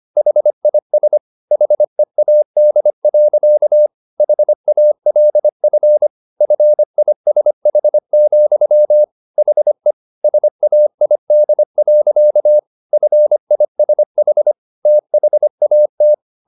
Morse code, heard as a keyed tone, about 25 wpm, 600 Hz. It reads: HIS HEAD. HALF FISH, HE SAID. FISH THAT